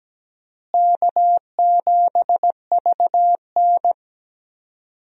Morse code, heard as a keyed tone, about 17 wpm, 700 Hz. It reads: K7VN
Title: USA Call Signs - 4 Characters - Repeated 3x - 17wpm